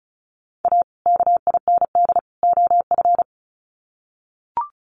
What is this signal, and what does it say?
Morse code, heard as a keyed tone, about 34 wpm, 700 Hz. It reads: A KIND OF
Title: Sets of 3 Words 35wpm